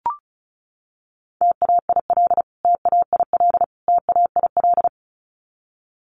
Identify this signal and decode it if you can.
Morse code, heard as a keyed tone, about 34 wpm, 700 Hz. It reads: TAIL TAIL TAIL
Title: Top 500 Words Repeated 35wpm